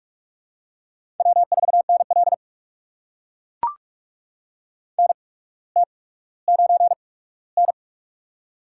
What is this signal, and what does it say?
Morse code, heard as a keyed tone, about 44 wpm, spaced as for 19 wpm, 700 Hz. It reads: W4NL NT9N
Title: USA Call Signs - 4 Characters - 8x Character Spacing - 45wpm